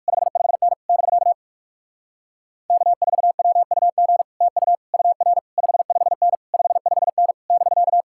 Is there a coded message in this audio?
Yes: Morse code from 0.1 to 8.0 s, heard as a keyed tone, about 59 wpm, 700 Hz. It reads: 55N <BK> K4WUG TU UR 55N 55N <BK>